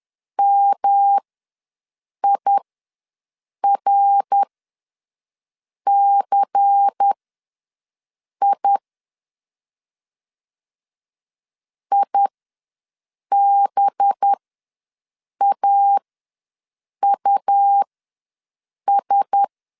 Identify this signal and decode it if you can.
Morse code, heard as a keyed tone, about 10 wpm, spaced as for 6 wpm, 800 Hz. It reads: MIRCI IBAUS
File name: LEZIONI / 5 LBVC / 5 LBVC ws 10wpm 3733